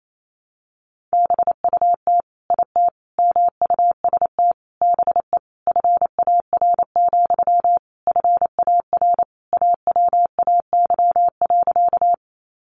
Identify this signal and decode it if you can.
Morse code, heard as a keyed tone, about 28 wpm, 700 Hz. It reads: BUT IT MUST BE FAR, FAR AWAY.
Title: Sentences from Top 600 Words 28wpm